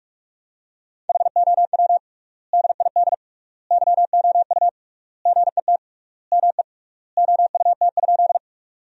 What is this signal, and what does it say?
Morse code, heard as a keyed tone, about 46 wpm, 700 Hz. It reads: HOW DID YOU GET ME OUT?
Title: 3x Word-Spacing - Sentences from Top 200 Words - 45wpm